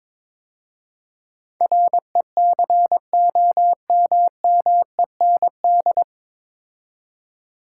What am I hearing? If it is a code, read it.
Morse code, heard as a keyed tone, about 22 wpm, 700 Hz. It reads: RECOMMEND
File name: 9 Letter Words - First 3 Letters Given - 22wpm